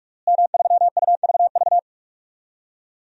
Morse code, heard as a keyed tone, about 41 wpm, 700 Hz. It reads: M3UVV
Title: DXCC-100 Call Signs - 5 Characters - Repeated 3x - 45wpm